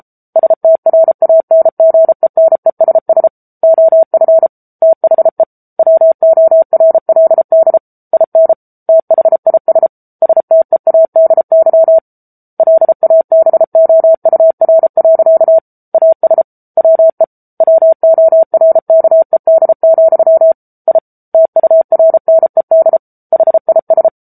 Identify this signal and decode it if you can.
Morse code, heard as a keyed tone, about 34 wpm, 650 Hz. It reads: STRANGENESS OF THE WORLD IN THIS STEADY LABOUR. AS WE WORKED, I TURNED HIS
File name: Book Two / 07-Chapter 7 - 33 WPM